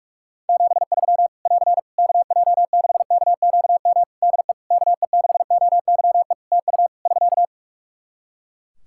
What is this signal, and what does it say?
Morse code, heard as a keyed tone, about 44 wpm, 700 Hz. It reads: 73 <AR> KJ6KQM DE KE6OYE TU <SK>
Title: Practice QSO - Exchange 4 - Ending - 45wpm